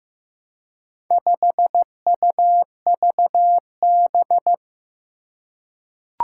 Morse code, heard as a keyed tone, about 15 wpm, 700 Hz. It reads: 5UVB
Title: Four Challenging Letter-Number Combinations 15wpm